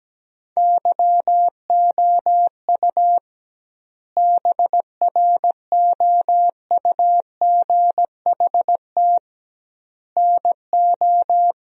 Morse code, heard as a keyed tone, about 17 wpm, 700 Hz. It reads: YOU BROUGHT NO
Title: 2x Word-Spacing - Sentences from Top 500 Words - 17wpm